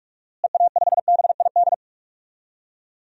Morse code, heard as a keyed tone, about 46 wpm, 700 Hz. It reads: EA5BID
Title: DXCC-100 Call Signs - 6 Characters - Repeated 2x - 45wpm